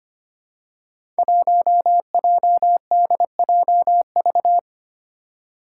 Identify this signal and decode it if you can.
Morse code, heard as a keyed tone, about 25 wpm, 700 Hz. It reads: 1JDJV
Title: Five Challenging Letter-Number Combinations 25wpm